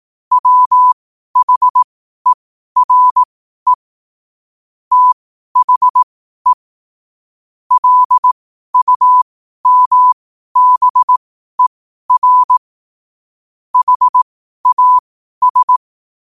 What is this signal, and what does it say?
Morse code, heard as a keyed tone, about 18 wpm, spaced as for 12 wpm, 1000 Hz. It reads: WHERE THE LUMBER HAS
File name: qotdfiles / 12wpm